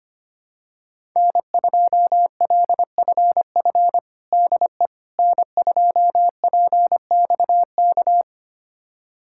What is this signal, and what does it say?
Morse code, heard as a keyed tone, about 25 wpm, 700 Hz. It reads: N2LFF DE N2PXK
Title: USA Callsign DE Callsign - 5 Characters - 25wpm